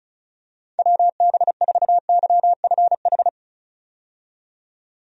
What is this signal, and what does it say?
Morse code, heard as a keyed tone, about 36 wpm, 700 Hz. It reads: WB4YFH
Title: USA Call Signs - 6 Characters - 35wpm